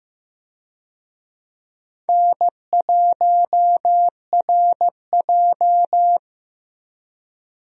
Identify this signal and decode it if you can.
Morse code, heard as a keyed tone, about 15 wpm, 700 Hz. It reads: N1RJ
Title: USA Call Signs - 4 Characters - 15wpm